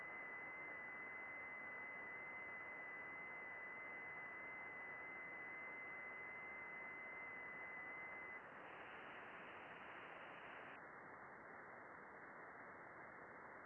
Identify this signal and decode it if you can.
FT8.
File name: contests / june-vhf-k5tr- -0200